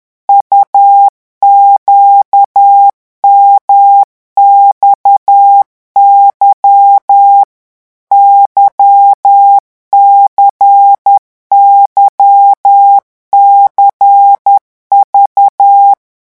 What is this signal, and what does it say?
Morse code, heard as a keyed tone, about 10 wpm, 800 Hz. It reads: UQMXY YCYCV
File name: morse mp3 / revision2-vitesse 10 mots